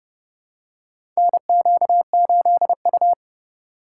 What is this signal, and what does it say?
Morse code, heard as a keyed tone, about 30 wpm, 690 Hz. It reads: NQ8U
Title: USA Call Signs - 4 Characters - 2x Character Spacing - 30wpm